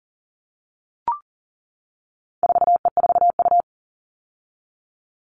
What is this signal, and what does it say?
Morse code, heard as a keyed tone, about 37 wpm, 700 Hz. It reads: 4E4U
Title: All Cut Numbers - 4-Digit 40wpm